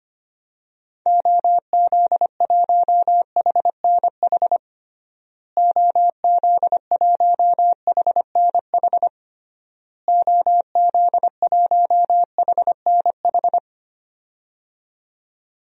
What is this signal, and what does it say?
Morse code, heard as a keyed tone, about 25 wpm, 700 Hz. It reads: OZ1HNH OZ1HNH OZ1HNH
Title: DXCC-100 Call Signs - 6 Characters - Repeated 3x - 25wpm